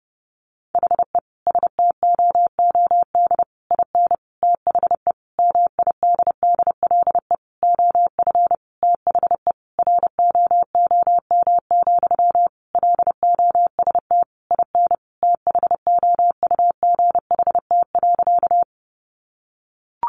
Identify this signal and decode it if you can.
Morse code, heard as a keyed tone, about 30 wpm, 700 Hz. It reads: HE STOOD IN THE MIDDLE OF THE ROOM, LOST IN THOUGHT.
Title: Sentences from Top 1400 Words 30wpm